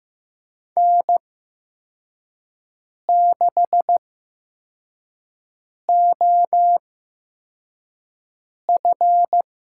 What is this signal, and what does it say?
Morse code, heard as a keyed tone, about 15 wpm, spaced as for 4 wpm, 700 Hz. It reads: N6OF
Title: USA Call Signs - 8x Character Spacing - 15wpm